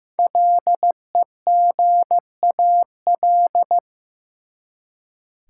Morse code, heard as a keyed tone, about 15 wpm, 700 Hz. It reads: LEGAL
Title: Five Letter Words - 15wpm